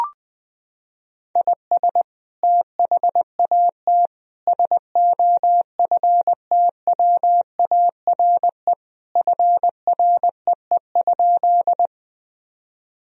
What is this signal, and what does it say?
Morse code, heard as a keyed tone, about 20 wpm, 700 Hz. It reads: IS THAT SOFTWARE FREE?
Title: Sentences from Top 1000 Words - Review Part 3 - 20wpm